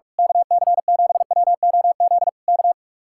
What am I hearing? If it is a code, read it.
Morse code, heard as a keyed tone, about 46 wpm, 700 Hz. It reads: KC7WOZ K
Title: Speed-Racing Common QSO Phrases - 45wpm